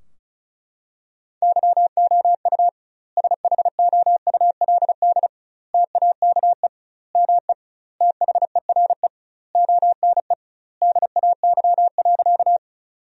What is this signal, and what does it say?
Morse code, heard as a keyed tone, about 36 wpm, 700 Hz. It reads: YOU SHOULD TAKE ME THERE ONE DAY.
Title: 2x Word-Spacing - Sentences from Top 300 Words - 35wpm